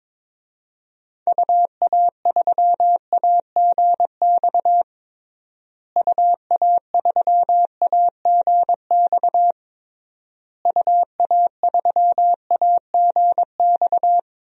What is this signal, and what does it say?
Morse code, heard as a keyed tone, about 21 wpm, 700 Hz. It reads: UA3AGX UA3AGX UA3AGX
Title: DXCC-100 Call Signs - 6 Characters - Repeated 3x - 22wpm